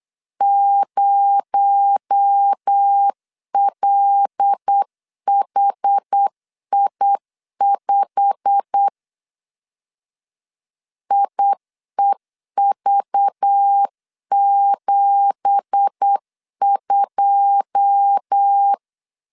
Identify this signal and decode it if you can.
Morse code, heard as a keyed tone, about 8 wpm, 800 Hz. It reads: 0LHI5 IEV72
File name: CW / DET / 1 9 1500C S A 8wpm 5637